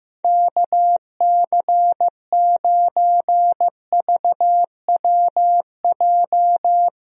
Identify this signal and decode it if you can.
Morse code, heard as a keyed tone, about 15 wpm, 700 Hz. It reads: KC9VWJ
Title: USA Call Signs - 6 Characters - Repeated 3x - 15wpm